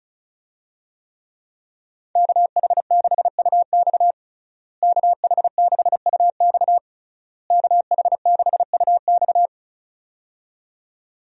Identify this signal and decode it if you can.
Morse code, heard as a keyed tone, about 34 wpm, 700 Hz. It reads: KH6UX KH6UX KH6UX
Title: USA Call Signs - 5 Characters - Repeated 3x - 35wpm